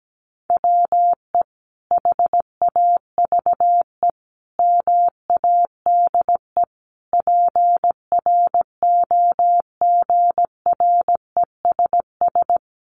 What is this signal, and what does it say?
Morse code, heard as a keyed tone, about 17 wpm, 700 Hz. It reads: WE HAVE MADE PROGRESS
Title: Sentences from Top 1400 Words 17wpm